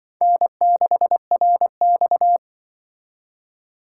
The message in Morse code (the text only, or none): N6RX